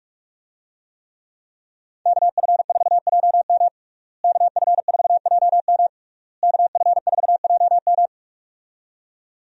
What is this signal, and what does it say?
Morse code, heard as a keyed tone, about 46 wpm, 700 Hz. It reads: KF4JM KF4JM KF4JM
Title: USA Call Signs - 5 Characters - Repeated 3x - 45wpm